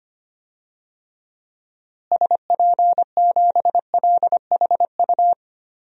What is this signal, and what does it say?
Morse code, heard as a keyed tone, about 25 wpm, 700 Hz. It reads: SP7LHU
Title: DXCC-100 Call Signs - 6 Characters - Repeated 3x - 25wpm